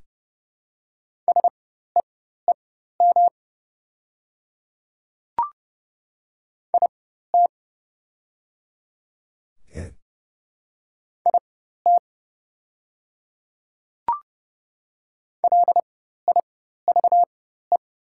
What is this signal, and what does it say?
Morse code, heard as a keyed tone, about 30 wpm, spaced as for 14 wpm, 700 Hz. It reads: SEEM IT IT LIVE
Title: 4x Character Spacing - Top 300 Words - 30wpm